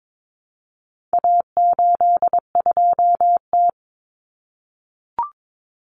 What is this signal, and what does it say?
Morse code, heard as a keyed tone, about 22 wpm, 700 Hz. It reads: A82T
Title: Common Cut Numbers - 4-Digit 22wpm